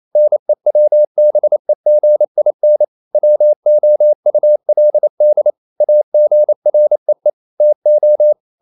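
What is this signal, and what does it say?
Morse code, heard as a keyed tone, about 28 wpm, 600 Hz. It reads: NEWBEGIN WOULD AGREE TO
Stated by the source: Morse code practice at 28 wpm, text of Ken Follett's novel Never in English, from CW med Gnister.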